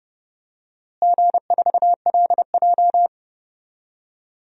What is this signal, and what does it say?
Morse code, heard as a keyed tone, about 30 wpm, 700 Hz. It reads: G4LJ
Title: DXCC-100 Call Signs - 4 Characters - Repeated 3x - 30wpm